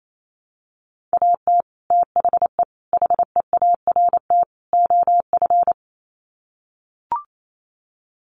Morse code, heard as a keyed tone, about 28 wpm, 700 Hz. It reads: AT THE HEART OF
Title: Sets of 4 Words 28wpm